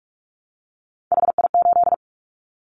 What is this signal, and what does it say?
Morse code, heard as a keyed tone, about 46 wpm, 700 Hz. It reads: HI8
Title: Callsign Prefixes for Top 100 DXCC - 45wpm